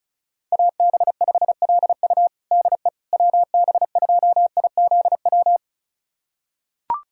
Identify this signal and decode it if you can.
Morse code, heard as a keyed tone, about 36 wpm, 700 Hz. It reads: AB5LU DE WB2IZW
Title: USA Callsign DE Callsign 35wpm